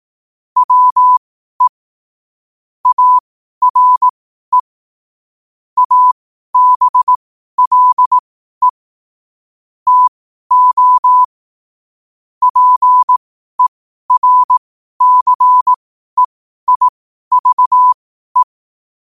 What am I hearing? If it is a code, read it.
Morse code, heard as a keyed tone, about 18 wpm, spaced as for 12 wpm, 1000 Hz. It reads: WE ARE ABLE TO PERCEIVE
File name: qotdfiles / 12wpm